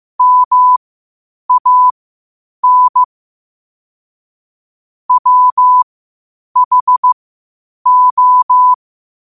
Morse code, heard as a keyed tone, about 15 wpm, spaced as for 8 wpm, 1000 Hz. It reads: MAN WHO
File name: qotdfiles / 8wpm